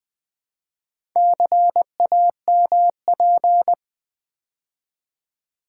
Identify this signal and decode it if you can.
Morse code, heard as a keyed tone, about 20 wpm, 700 Hz. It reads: CAMP
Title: Build A Word - 2 Letters At A Time - 8-Letter Words - 20wpm